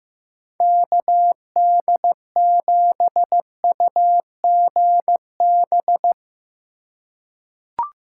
Morse code, heard as a keyed tone, about 15 wpm, 700 Hz. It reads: KD7UGB
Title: USA Call Signs - 6 Characters - 4x Character Spacing - 15wpm